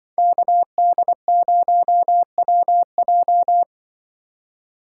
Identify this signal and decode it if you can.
Morse code, heard as a keyed tone, about 25 wpm, 700 Hz. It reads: KD0WJ
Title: Speed Racing - USA Call Signs 20wpm